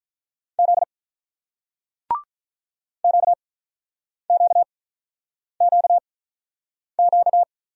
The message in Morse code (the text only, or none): Z QQQQ